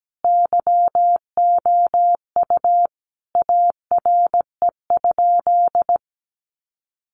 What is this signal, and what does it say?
Morse code, heard as a keyed tone, about 17 wpm, 700 Hz. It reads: YOU ARE?
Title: Sentences from Top 1400 Words 17wpm